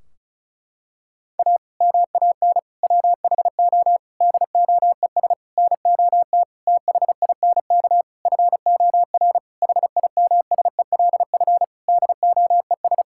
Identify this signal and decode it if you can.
Morse code, heard as a keyed tone, about 36 wpm, 700 Hz. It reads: A MAN WHO DOES NOT THINK FOR HIMSELF DOES
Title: Sentences from Top 1600 Words 35wpm